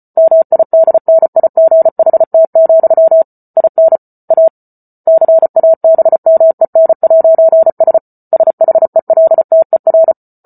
Morse code, heard as a keyed tone, about 34 wpm, 650 Hz. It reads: MIDNIGHT, IN A CABMEN'S SHELTER